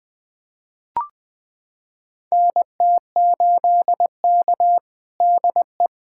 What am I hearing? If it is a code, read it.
Morse code, heard as a keyed tone, about 20 wpm, 700 Hz. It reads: NT8K DE
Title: USA Callsign DE Callsign - 4 Characters - 20wpm